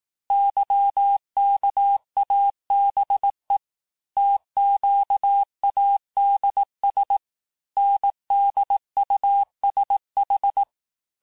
Morse code, heard as a keyed tone, about 18 wpm, 800 Hz. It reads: YKABE TQADS NDUSH
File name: cours-du-lundi files / cours-1588-texte-chiffre-vitesse-17MPM